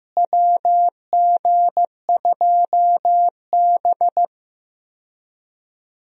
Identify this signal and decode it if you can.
Morse code, heard as a keyed tone, about 15 wpm, 700 Hz. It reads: WG2B
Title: USA Call Signs - 4 Characters - Repeated 2x - 15wpm